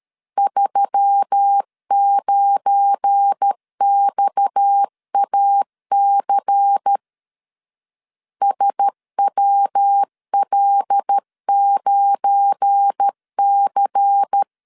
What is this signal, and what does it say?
Morse code, heard as a keyed tone, about 13 wpm, 800 Hz. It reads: 39XAC SWL9C
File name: LEZIONI / 7 GF9Z / 7 GF9 12wpm 2319